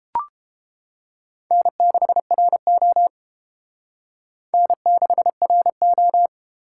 Morse code, heard as a keyed tone, about 28 wpm, 700 Hz. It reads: N6RO N6RO